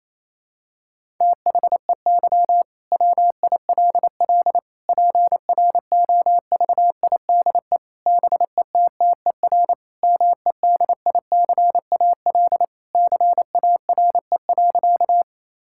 Morse code, heard as a keyed tone, about 28 wpm, 700 Hz. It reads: THEY WILL PROVIDE BETTER MEDICAL CARE.
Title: Sentences from Top 1000 Words - Review Part 5 - 28wpm